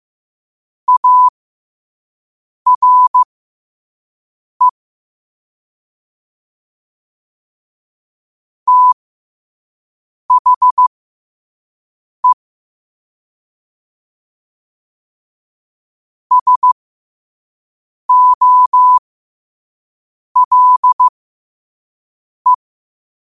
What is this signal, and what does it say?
Morse code, heard as a keyed tone, about 15 wpm, spaced as for 5 wpm, 1000 Hz. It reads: ARE THE SOLE
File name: qotdfiles / 5wpm